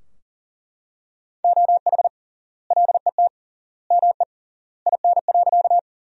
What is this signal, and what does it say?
Morse code, heard as a keyed tone, about 37 wpm, 700 Hz. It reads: OH LET ME IN.
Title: 3x Word-Spacing - Sentences from Top 500 Words - 40wpm